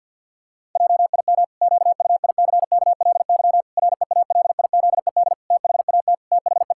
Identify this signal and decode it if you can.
Morse code, heard as a keyed tone, about 50 wpm, 700 Hz. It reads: JIM QUICKLY REALIZED THAT THE
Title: Pangram Challenge v2 - 50wpm